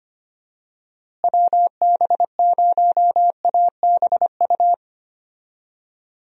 Morse code, heard as a keyed tone, about 25 wpm, 700 Hz. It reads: WB0ABU
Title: USA Call Signs - 6 Characters - Repeated 3x - 25wpm